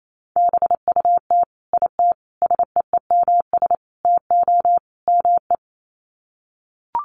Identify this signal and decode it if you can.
Morse code, heard as a keyed tone, about 28 wpm, 700 Hz. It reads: BUT IT SEEMS TO ME
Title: Sets of 5 Words Encore2 28wpm